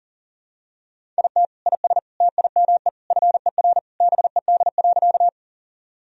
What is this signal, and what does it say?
Morse code, heard as a keyed tone, about 40 wpm, 700 Hz. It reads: IT IS TIME FER BED.
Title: Sentences from Top 500 Words and Abbreviations 40wpm